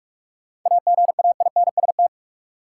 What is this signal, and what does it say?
Morse code, heard as a keyed tone, about 46 wpm, 700 Hz. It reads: AGAINST
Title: Sentences from Top 400 Words - New Word Drill - 45wpm